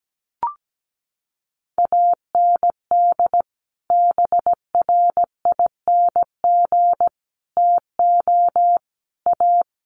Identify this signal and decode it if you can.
Morse code, heard as a keyed tone, about 17 wpm, 700 Hz. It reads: AND BRING TO A
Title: Sets of 5 Words Encore 17wpm